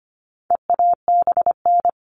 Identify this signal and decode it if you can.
Morse code, heard as a keyed tone, about 25 wpm, 700 Hz. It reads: EABN
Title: All Cut Numbers - 4-Digit 25wpm